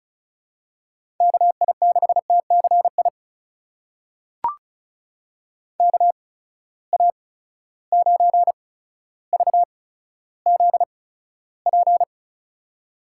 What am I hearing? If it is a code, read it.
Morse code, heard as a keyed tone, about 36 wpm, spaced as for 15 wpm, 700 Hz. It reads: KI6TCI KA9VZP
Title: USA Call Signs - 6 Characters - 8x Character Spacing - 35wpm